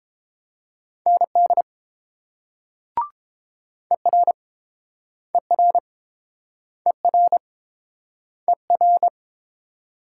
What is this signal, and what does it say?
Morse code, heard as a keyed tone, about 30 wpm, 700 Hz. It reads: ND ER ER ER ER